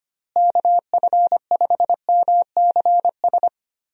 Morse code, heard as a keyed tone, about 25 wpm, 700 Hz. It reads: KF5MCS
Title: USA Call Signs - 6 Characters - Repeated 2x - 25wpm